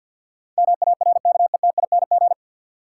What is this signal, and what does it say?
Morse code, heard as a keyed tone, about 50 wpm, 700 Hz. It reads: MARKETING
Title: Sentences from Top 1000 Words - Review Part 5 - New Word Drill - 50wpm